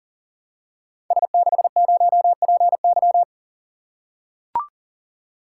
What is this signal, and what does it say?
Morse code, heard as a keyed tone, about 40 wpm, 700 Hz. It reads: S60PY
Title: Code Groups - 5 Letter:Numbers - 40wpm